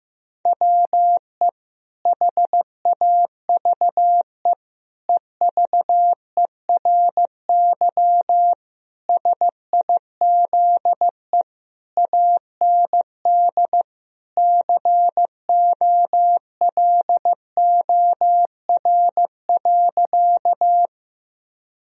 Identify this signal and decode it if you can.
Morse code, heard as a keyed tone, about 15 wpm, 700 Hz. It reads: WE HAVE EVERY SIZE AND COLOR.
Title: Sentences from Top 1000 Words - Review Part 2 - 15wpm